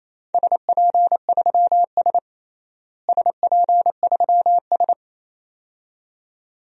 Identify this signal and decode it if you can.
Morse code, heard as a keyed tone, about 28 wpm, 700 Hz. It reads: SP3S SP3S
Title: DXCC-100 Call Signs - 4 Characters - Repeated 2x - 28wpm